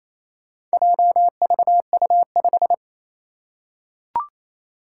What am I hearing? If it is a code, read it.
Morse code, heard as a keyed tone, about 28 wpm, 700 Hz. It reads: JVU5
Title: Four Challenging Letter-Number Combinations 28wpm